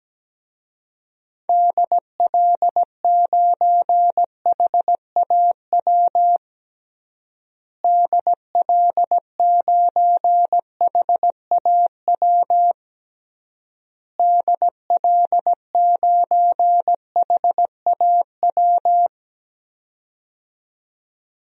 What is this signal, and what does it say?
Morse code, heard as a keyed tone, about 17 wpm, 700 Hz. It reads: DL9HAW DL9HAW DL9HAW
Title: DXCC-100 Call Signs - 6 Characters - Repeated 3x - 17wpm